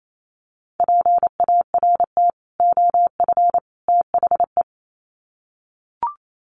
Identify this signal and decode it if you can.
Morse code, heard as a keyed tone, about 28 wpm, 700 Hz. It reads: PART OF THE
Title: Sets of 3 Words 28wpm